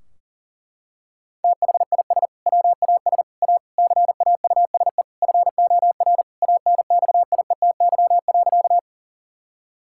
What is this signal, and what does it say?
Morse code, heard as a keyed tone, about 40 wpm, 700 Hz. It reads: THIS WAS A CAUSE FOR ANXIETY.
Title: Sentences from Top 1600 Words 40wpm